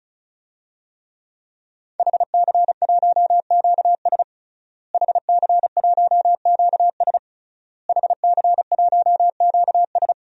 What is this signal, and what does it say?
Morse code, heard as a keyed tone, about 36 wpm, 700 Hz. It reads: HC1QS HC1QS HC1QS
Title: DXCC-100 Call Signs - 5 Characters - Repeated 3x - 35wpm